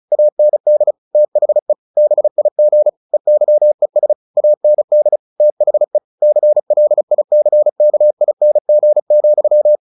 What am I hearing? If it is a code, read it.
Morse code, heard as a keyed tone, about 34 wpm, 600 Hz. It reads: AND THE BIG EYES AND THE CLICKING,
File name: the old man and the sea 2 35wpm